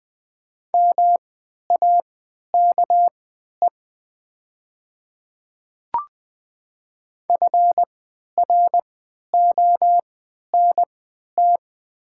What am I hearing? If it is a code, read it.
Morse code, heard as a keyed tone, about 20 wpm, spaced as for 11 wpm, 700 Hz. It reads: MAKE FRONT
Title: 3x Character Spacing - Top 500 Words - 20wpm